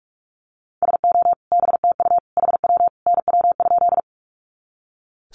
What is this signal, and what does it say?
Morse code, heard as a keyed tone, about 44 wpm, 700 Hz. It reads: SO BTU HW NW?
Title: Common QSO Phrases 45wpm